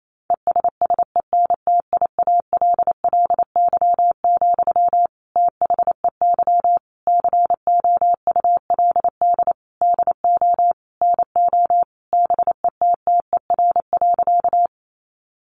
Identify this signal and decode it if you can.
Morse code, heard as a keyed tone, about 28 wpm, 700 Hz. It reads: ESSENTIALLY, THEY COULD DO NO BETTER.
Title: Sentences from Top 1300 Words 28wpm